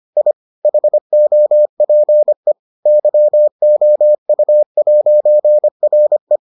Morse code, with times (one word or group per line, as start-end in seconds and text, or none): 0.2-0.3 I
0.6-2.5 HOPE
2.9-6.4 YOU'RE